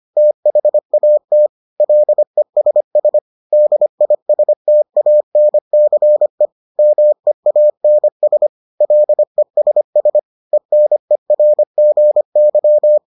Morse code, heard as a keyed tone, about 25 wpm, 600 Hz. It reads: THAT LESS DISTANCE MEANS LESS ENERGY